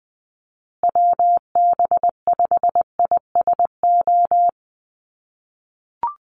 Morse code, heard as a keyed tone, about 20 wpm, 700 Hz. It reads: WB5ISO
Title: USA Call Signs Repeated 3x 20wpm